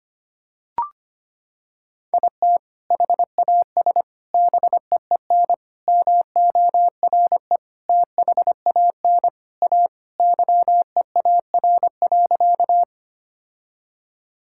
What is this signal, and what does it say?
Morse code, heard as a keyed tone, about 25 wpm, 700 Hz. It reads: IT HAS BEEN MORE THAN A YEAR.
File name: practice / 25wpm / Sentences with 4-Word Phrases - 25wpm